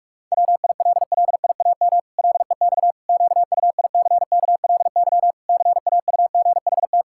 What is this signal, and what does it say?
Morse code, heard as a keyed tone, about 44 wpm, 700 Hz. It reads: WILLIAM LEX QUICKLY CAUGHT
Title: Pangram Challenge v2 - 45wpm